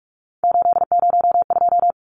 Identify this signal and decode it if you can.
Morse code, heard as a keyed tone, about 44 wpm, 700 Hz. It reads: 802
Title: Numbers - 3-Digit 45wpm